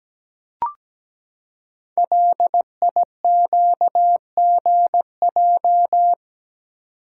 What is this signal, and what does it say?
Morse code, heard as a keyed tone, about 17 wpm, 700 Hz. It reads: LIQGJ
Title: Code Groups - 5 Letters - 17wpm